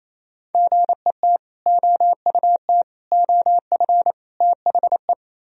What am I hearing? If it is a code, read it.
Morse code, heard as a keyed tone, about 35 wpm, 700 Hz. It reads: GET OUT OF THE